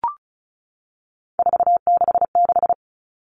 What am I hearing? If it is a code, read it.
Morse code, heard as a keyed tone, about 34 wpm, 700 Hz. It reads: 466
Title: All Cut Numbers - 3-Digit 35wpm